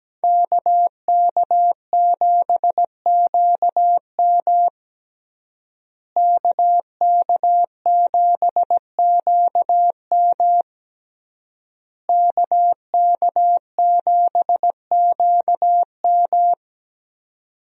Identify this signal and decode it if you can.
Morse code, heard as a keyed tone, about 17 wpm, 700 Hz. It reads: KK7QM KK7QM KK7QM
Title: USA Call Signs - 5 Characters - Repeated 3x - 17wpm